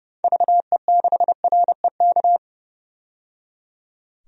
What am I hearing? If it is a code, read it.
Morse code, heard as a keyed tone, about 30 wpm, 700 Hz. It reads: VE6REK
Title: DXCC-100 Call Signs - 6 Characters - 30wpm